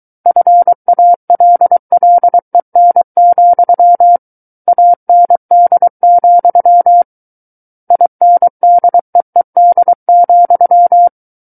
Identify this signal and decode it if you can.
Morse code, heard as a keyed tone, about 23 wpm, 700 Hz. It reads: FALLEN, AND, INDEED,
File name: Book One / 14-Chapter 14 - 23 WPM